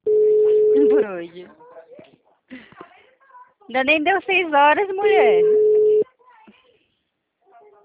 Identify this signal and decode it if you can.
Ringback tone (Europe, 425 Hz), with speech.
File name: out- -2049- - - .15254